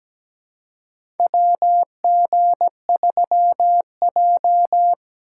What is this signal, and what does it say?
Morse code, heard as a keyed tone, about 17 wpm, 700 Hz. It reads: WG3J
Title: USA Call Signs - 4 Characters - 3x Character Spacing - 17wpm